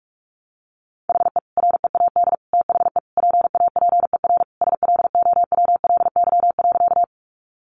Morse code, heard as a keyed tone, about 46 wpm, 700 Hz. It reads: HE READ THE PAPER SLOWLY.
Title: Sentences from Top 1100 Words 45wpm